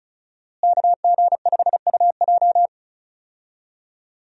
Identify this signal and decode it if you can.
Morse code, heard as a keyed tone, about 34 wpm, 700 Hz. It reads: KG5UJ
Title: USA Call Signs - 5 Characters - Repeated 3x - 35wpm